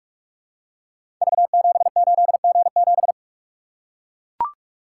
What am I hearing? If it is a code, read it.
Morse code, heard as a keyed tone, about 43 wpm, 700 Hz. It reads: V78G7
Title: Code Groups - 5 Letter:Numbers - 45wpm